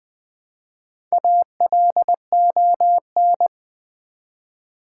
Morse code, heard as a keyed tone, about 20 wpm, 700 Hz. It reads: ALON
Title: Build A Word - 1 Letter At A Time - 3 to 5-Letter Words - 20wpm